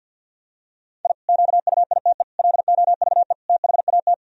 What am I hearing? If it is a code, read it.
Morse code, heard as a keyed tone, about 51 wpm, 700 Hz. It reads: I QUITE LOVE THAT
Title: Sentences from Top 1000 Words - Review Part 1 - 50wpm